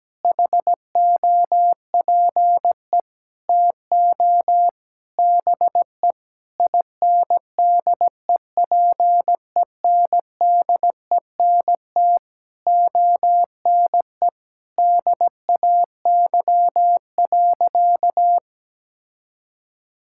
Morse code, heard as a keyed tone, about 17 wpm, 700 Hz. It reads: HOPE TO BE INDEPENDENT ONE DAY.
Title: Sentences from Top 1000 Words - Review Part 6 - 17wpm